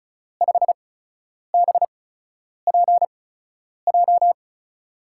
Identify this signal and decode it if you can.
Morse code, heard as a keyed tone, about 36 wpm, spaced as for 10 wpm, 700 Hz. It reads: 5BPJ
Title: USA Call Signs - 5 Characters - 8x Character Spacing - 35wpm